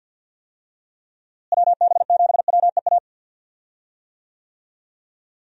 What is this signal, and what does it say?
Morse code, heard as a keyed tone, about 46 wpm, 700 Hz. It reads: WB7WEA
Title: USA Call Signs - 6 Characters - Repeated 2x - 50wpm